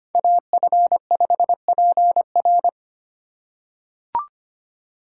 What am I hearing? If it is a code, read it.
Morse code, heard as a keyed tone, about 25 wpm, 700 Hz. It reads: AF5PR
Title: USA Call Signs - 5 Characters - 2x Character Spacing - 25wpm